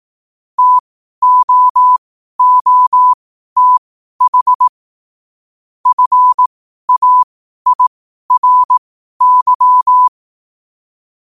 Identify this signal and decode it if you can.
Morse code, heard as a keyed tone, about 18 wpm, spaced as for 12 wpm, 1000 Hz. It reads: TOOTH FAIRY
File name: qotdfiles / 12wpm